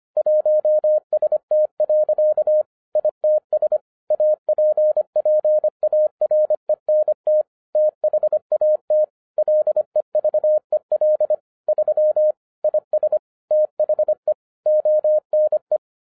Morse code, heard as a keyed tone, about 24 wpm, 610 Hz. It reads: 1ST. ITS APPARENT THAT LEVEL 3 IS THE ONE
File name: 2012-01-mp3 / n14-FUN WITH NUMBERS-25wpm